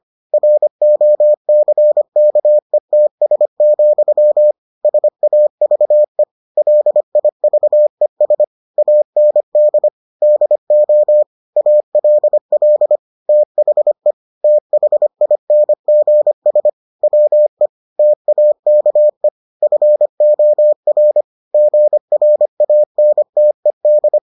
Morse code, heard as a keyed tone, about 25 wpm, 600 Hz. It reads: ROCKETS, SAVE LIVES AND DO ALL THE THINGS WE TAKE FOR GRANTED